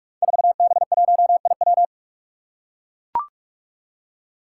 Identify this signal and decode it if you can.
Morse code, heard as a keyed tone, about 43 wpm, 700 Hz. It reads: 4B1IW
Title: Five Challenging Letter-Number Combinations 45wpm